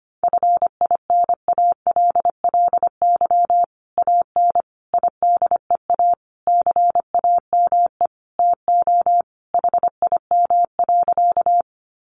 Morse code, heard as a keyed tone, about 25 wpm, 700 Hz. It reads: FINALLY AN IDEA CAME TO HIM.
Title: Sentences from Top 1000 Words 25wpm